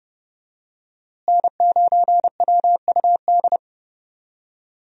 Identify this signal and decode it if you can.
Morse code, heard as a keyed tone, about 30 wpm, 700 Hz. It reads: N9WUD
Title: USA Call Signs - 5 Characters - Repeated 3x - 30wpm